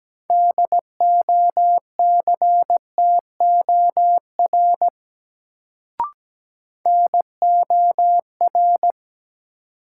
Morse code, heard as a keyed tone, about 17 wpm, 700 Hz. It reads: DOCTOR NOR
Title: Sentences from Top 1500 Words - New Word Drill - 17wpm